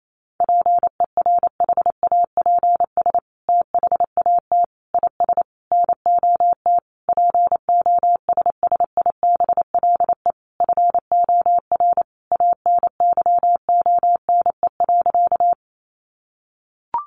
Morse code, heard as a keyed tone, about 28 wpm, 700 Hz. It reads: PERHAPS THAT IS NOT POSSIBLE FOR ANYONE.
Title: Sentences from Top 900 Words 28wpm